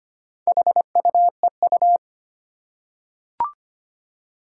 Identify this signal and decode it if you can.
Morse code, heard as a keyed tone, about 25 wpm, 700 Hz. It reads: HUEU
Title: Four Challenging Letter-Number Combinations 25wpm